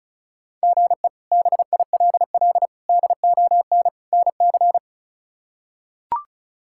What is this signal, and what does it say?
Morse code, heard as a keyed tone, about 34 wpm, 700 Hz. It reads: GE BILL DON NC
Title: K1USN SST QSO Practice - 35wpm